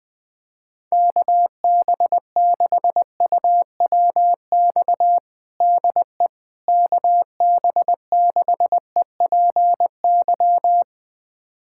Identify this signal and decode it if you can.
Morse code, heard as a keyed tone, about 20 wpm, 700 Hz. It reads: KB6UWX DE KB6EPY
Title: USA Callsign DE Callsign 20wpm